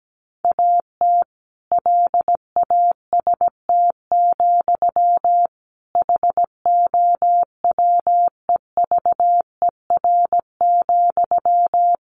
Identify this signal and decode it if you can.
Morse code, heard as a keyed tone, about 17 wpm, 700 Hz. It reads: AT LAST, HOWEVER,
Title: Sentences from Top 1500 Words 17wpm